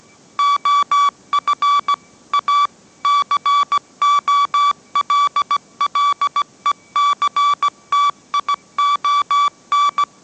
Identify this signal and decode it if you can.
Morse code, heard as a keyed tone, about 17 wpm, 1150 Hz. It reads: OF A COLLECTION
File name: mrx / mrx080 18wpm